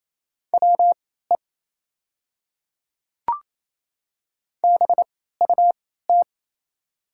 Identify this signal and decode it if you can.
Morse code, heard as a keyed tone, about 28 wpm, spaced as for 16 wpm, 700 Hz. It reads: WE BUT
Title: 3x Character Spacing - Top 100 Words - 28wpm